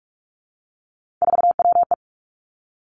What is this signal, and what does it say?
Morse code, heard as a keyed tone, about 46 wpm, 700 Hz. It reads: 4WE
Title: Three Challenging Letter-Number Combinations 45wpm